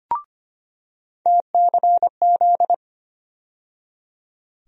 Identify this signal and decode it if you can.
Morse code, heard as a keyed tone, about 25 wpm, 700 Hz. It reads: TCZ
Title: USA Call Sign Postfixes - 3 Letters 25wpm